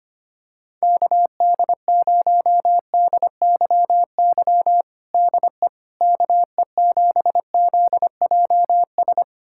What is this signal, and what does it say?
Morse code, heard as a keyed tone, about 25 wpm, 700 Hz. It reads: KD0DYY DE KE7ZJS
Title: USA Callsign DE Callsign - 6 Characters - 25wpm